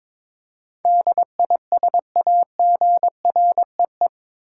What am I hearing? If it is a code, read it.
Morse code, heard as a keyed tone, about 22 wpm, 700 Hz. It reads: DISAGREE
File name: practice / 22wpm / Eight Letter Words - 22wpm